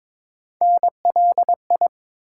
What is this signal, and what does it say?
Morse code, heard as a keyed tone, about 22 wpm, 700 Hz. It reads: NLI
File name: ARRL Sections v2 - 22wpm